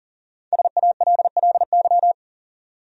Morse code, heard as a keyed tone, about 40 wpm, 700 Hz. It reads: SALLY
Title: USA Common Names - Max 5 Letters - 40wpm